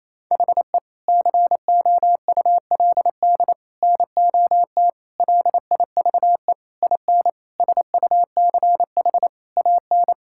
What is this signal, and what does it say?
Morse code, heard as a keyed tone, about 28 wpm, 700 Hz. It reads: HE COULD NOT LIVE IN SUCH AN
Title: Sentences from Top 1600 Words 28wpm